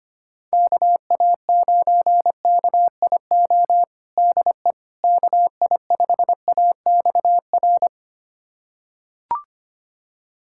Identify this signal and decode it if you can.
Morse code, heard as a keyed tone, about 25 wpm, 700 Hz. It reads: KA9KIO DE KI5AXR
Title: USA Callsign DE Callsign - 6 Characters - 25wpm